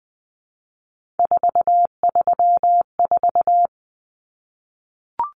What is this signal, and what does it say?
Morse code, heard as a keyed tone, about 20 wpm, 700 Hz. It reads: 434
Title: Common Cut Numbers - 3-Digit 20wpm